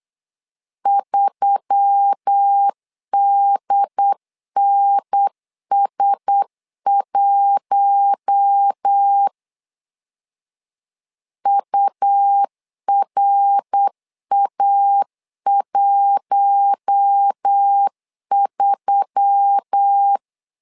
Morse code, heard as a keyed tone, about 8 wpm, 800 Hz. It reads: 3DNS1 URA13
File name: LEZIONI / 4 Q1S3 8wpm 3037